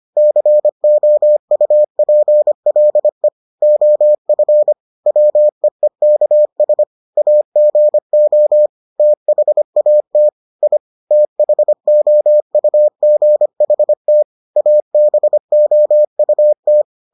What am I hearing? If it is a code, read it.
Morse code, heard as a keyed tone, about 25 wpm, 600 Hz. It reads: COUPLE OF WEEKS AGO THAT I THOUGHT ABOUT